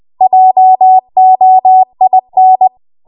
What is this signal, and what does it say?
Morse code, heard as a keyed tone, about 20 wpm, 750 Hz. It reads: JOIN
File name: Morse / 20WPM